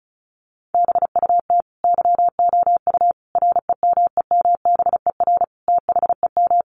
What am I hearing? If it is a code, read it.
Morse code, heard as a keyed tone, about 34 wpm, 700 Hz. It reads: BUT YOU REMEMBER THEM
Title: Sentences from Top 1300 Words 35wpm